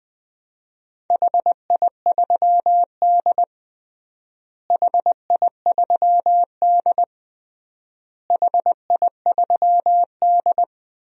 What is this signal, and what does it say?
Morse code, heard as a keyed tone, about 20 wpm, 700 Hz. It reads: HI3D HI3D HI3D
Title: DXCC-100 Call Signs - 4 Characters - Repeated 3x - 20wpm